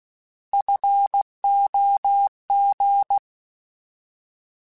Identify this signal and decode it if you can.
Morse code, heard as a keyed tone, about 16 wpm, 800 Hz. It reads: FOG
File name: Alphabetical / Alphabetical mp3s / Fs @15